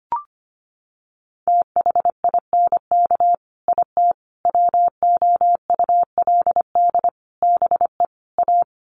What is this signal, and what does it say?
Morse code, heard as a keyed tone, about 25 wpm, 700 Hz. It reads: THINK IT WOULD BE A
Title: Sets of 5 Words Encore2 25wpm